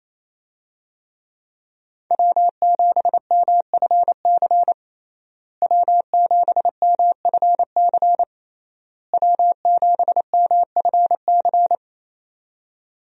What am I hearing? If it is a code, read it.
Morse code, heard as a keyed tone, about 28 wpm, 700 Hz. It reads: W7MFC W7MFC W7MFC
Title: USA Call Signs - 5 Characters - Repeated 3x - 28wpm